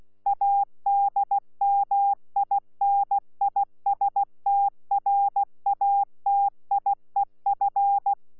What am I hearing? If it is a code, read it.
Morse code, heard as a keyed tone, about 16 wpm, 800 Hz. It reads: ADMINISTRATIEF
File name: Z PI4VRZ / PI4VRZ 14-jun-2025